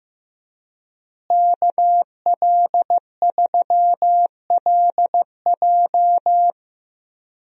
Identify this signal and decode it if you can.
Morse code, heard as a keyed tone, about 15 wpm, 700 Hz. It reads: KL3LJ
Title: USA Call Signs - 5 Characters - Repeated 2x - 15wpm